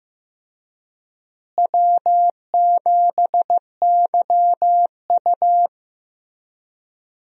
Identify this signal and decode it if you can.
Morse code, heard as a keyed tone, about 15 wpm, 700 Hz. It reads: W7YU
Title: USA Call Signs - 4 Characters - Repeated 3x - 15wpm